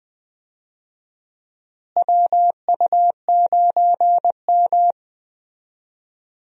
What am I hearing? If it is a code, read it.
Morse code, heard as a keyed tone, about 20 wpm, 700 Hz. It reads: WU9M
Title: USA Call Signs - 4 Characters - Repeated 2x - 20wpm